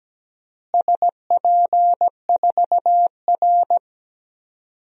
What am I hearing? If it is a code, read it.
Morse code, heard as a keyed tone, about 17 wpm, 700 Hz. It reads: SP4R
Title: DXCC-100 Call Signs - 4 Characters - Repeated 2x - 17wpm